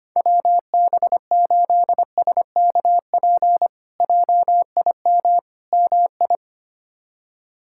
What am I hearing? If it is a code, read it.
Morse code, heard as a keyed tone, about 25 wpm, 700 Hz. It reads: WB8SKP JIM MI
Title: K1USN SST QSO Practice - 25wpm